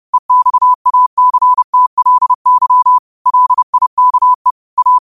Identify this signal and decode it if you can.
Morse code, heard as a keyed tone, about 30 wpm, 1000 Hz. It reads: EXACTLY LIKE A
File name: qotdfiles / 30wpm